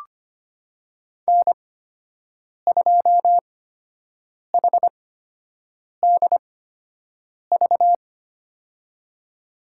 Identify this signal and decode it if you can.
Morse code, heard as a keyed tone, about 25 wpm, spaced as for 7 wpm, 700 Hz. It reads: N2HDV
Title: USA Call Signs - 8x Character Spacing - 25wpm